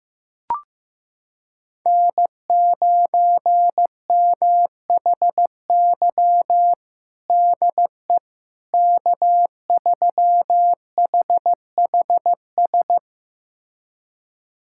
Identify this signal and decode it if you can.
Morse code, heard as a keyed tone, about 15 wpm, 700 Hz. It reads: N9MHY DE K3HHS
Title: USA Callsign DE Callsign - 5 Characters - 15wpm